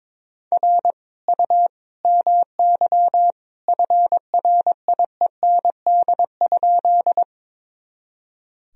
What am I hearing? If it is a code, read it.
Morse code, heard as a keyed tone, about 22 wpm, 700 Hz. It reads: R U MY FRIEND?
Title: Sentences from Top 300 Words and Abbreviations 22wpm